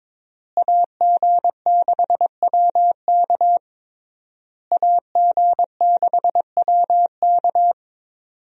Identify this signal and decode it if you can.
Morse code, heard as a keyed tone, about 22 wpm, 700 Hz. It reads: AG6WK AG6WK
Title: USA Call Signs - 5 Characters - Repeated 3x - 22wpm